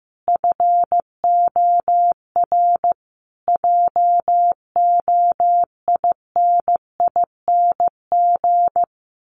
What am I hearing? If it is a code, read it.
Morse code, heard as a keyed tone, about 15 wpm, 700 Hz. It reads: FOR JOINING
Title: Sets of 5 Words Encore2 15wpm